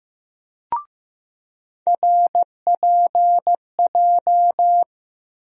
Morse code, heard as a keyed tone, about 15 wpm, 700 Hz. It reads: RPJ
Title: USA Call Sign Postfixes - 3 Letters 15wpm